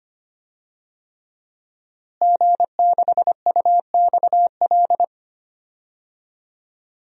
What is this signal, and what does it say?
Morse code, heard as a keyed tone, about 25 wpm, 700 Hz. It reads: G6UXL
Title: DXCC-100 Call Signs - 5 Characters - Repeated 3x - 25wpm